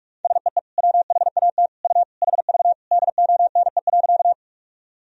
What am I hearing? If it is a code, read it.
Morse code, heard as a keyed tone, about 46 wpm, 700 Hz. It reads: SEE WHAT U HV DONE.
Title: Sentences from Top 500 Words and Abbreviations 45wpm